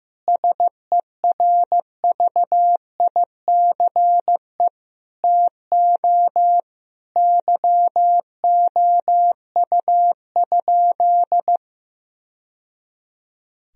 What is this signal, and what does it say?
Morse code, heard as a keyed tone, about 15 wpm, 700 Hz. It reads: SERVICE TO YOU?
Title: Sentences from Top 1000 Words - Review Part 4 - 15wpm